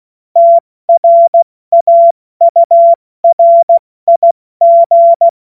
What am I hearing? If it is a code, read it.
Morse code, heard as a keyed tone, about 16 wpm, 680 Hz. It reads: TRAURIG
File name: Witz-08-Juli-2025-70-F80